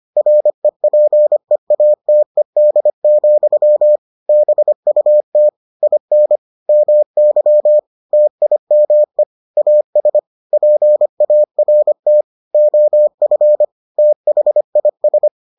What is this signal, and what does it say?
Morse code, heard as a keyed tone, about 25 wpm, 600 Hz. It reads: REPEATED, BUT IN MY TIME AS PART OF THIS